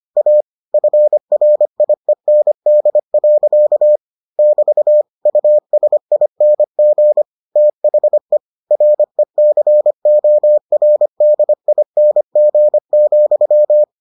Morse code, heard as a keyed tone, about 25 wpm, 600 Hz. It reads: A FRIEND. <BT> USING THE RECORDING,